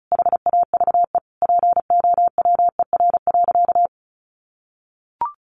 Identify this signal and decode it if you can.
Morse code, heard as a keyed tone, about 34 wpm, 700 Hz. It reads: HAVE POWER.
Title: Sentences from Top 1100 Words 35wpm